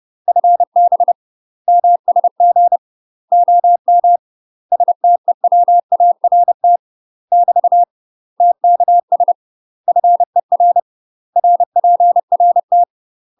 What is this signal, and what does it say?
Morse code, heard as a keyed tone, about 30 wpm, 700 Hz. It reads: FB MSG OM STEWART <BT> TKS FER RPRT
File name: UK MASTER QSOs TRAINING / 30WPM / UK MASTER QSOs 0042